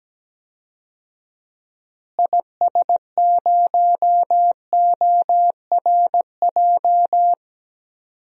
Morse code, heard as a keyed tone, about 17 wpm, 700 Hz. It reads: IS0ORJ
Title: DXCC-100 Call Signs - 6 Characters - Repeated 2x - 17wpm